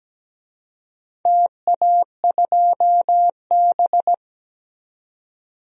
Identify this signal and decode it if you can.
Morse code, heard as a keyed tone, about 17 wpm, 700 Hz. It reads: TA2B
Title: DXCC-100 Call Signs - 4 Characters - Repeated 2x - 17wpm